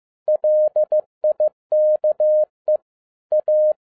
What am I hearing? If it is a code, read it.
Morse code, heard as a keyed tone, about 15 wpm, 610 Hz. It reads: LIKE A
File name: 2012-01-mp3 / n07-ARA JR0DIL-15wpm